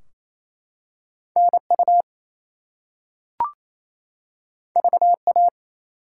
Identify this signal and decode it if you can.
Morse code, heard as a keyed tone, about 28 wpm, 700 Hz. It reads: NU VA
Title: US States and Canadian Provinces 28wpm